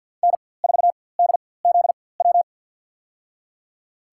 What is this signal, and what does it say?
Morse code, heard as a keyed tone, about 52 wpm, spaced as for 23 wpm, 700 Hz. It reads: N4DZW
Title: USA Call Signs - 5 Characters - 4x Character Spacing - 50wpm